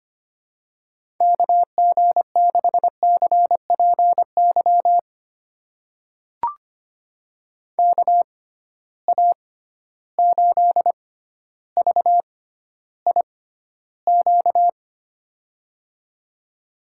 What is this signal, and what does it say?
Morse code, heard as a keyed tone, about 25 wpm, spaced as for 13 wpm, 700 Hz. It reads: KG6CPY KA8VIQ
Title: USA Call Signs - 6 Characters - 6x Character Spacing - 25wpm